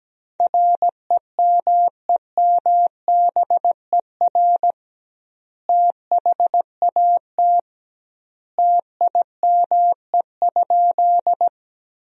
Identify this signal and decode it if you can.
Morse code, heard as a keyed tone, about 17 wpm, 700 Hz. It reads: REMEMBER THAT TIME?
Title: 2x Word-Spacing - Sentences from Top 400 Words - 17wpm